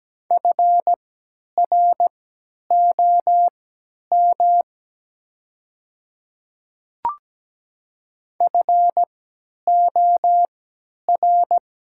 Morse code, heard as a keyed tone, about 17 wpm, spaced as for 10 wpm, 700 Hz. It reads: FROM FOR
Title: 3x Character Spacing - Top 200 Words - 17wpm